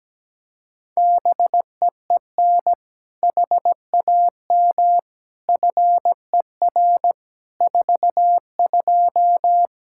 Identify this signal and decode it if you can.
Morse code, heard as a keyed tone, about 17 wpm, 700 Hz. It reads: BEEN HAM FER 42
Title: Practice QSO - Exchange 3 - More Info - 17wpm